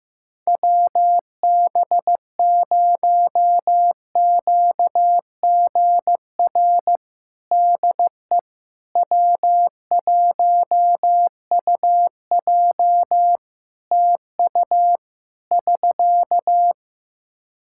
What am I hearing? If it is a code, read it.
Morse code, heard as a keyed tone, about 16 wpm, 700 Hz. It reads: WB0QGR DE W1UJ TU <SK>
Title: Practice QSO - Exchange 4 - Ending - 15wpm